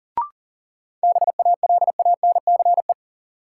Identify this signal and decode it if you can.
Morse code, heard as a keyed tone, about 40 wpm, 700 Hz. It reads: BALANCE
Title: Sentences from Top 900 Words - New Word Drill - 40wpm